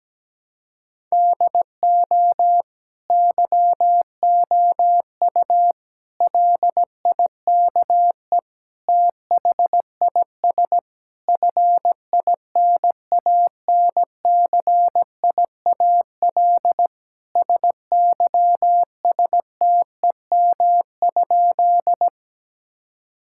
Morse code, heard as a keyed tone, about 17 wpm, 700 Hz. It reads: DO YOU LIKE THIS FINANCIAL SYSTEM?
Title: Sentences from Top 1000 Words - Review Part 5 - 17wpm